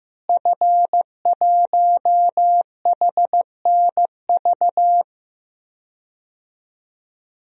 Morse code, heard as a keyed tone, about 15 wpm, 700 Hz. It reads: F1HNV
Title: DXCC-100 Call Signs - 5 Characters - Repeated 2x - 15wpm